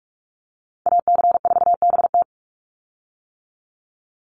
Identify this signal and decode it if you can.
Morse code, heard as a keyed tone, about 44 wpm, 700 Hz. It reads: AC4BT
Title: Call Signs - 45wpm